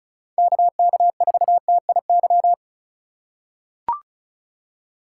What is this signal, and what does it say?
Morse code, heard as a keyed tone, about 36 wpm, 700 Hz. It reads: KK4TIY
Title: USA Call Signs - 6 Characters - 3x Character Spacing - 35wpm